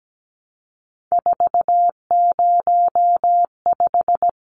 Morse code, heard as a keyed tone, about 17 wpm, 700 Hz. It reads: 405
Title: Numbers - 3-Digit 17wpm